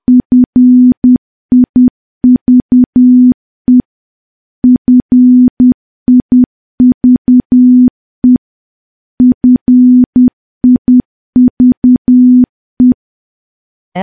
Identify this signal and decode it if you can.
Morse code, heard as a keyed tone, about 10 wpm, 250 Hz. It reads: FIVE FIVE FIVE
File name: morse / morse-training-10WPM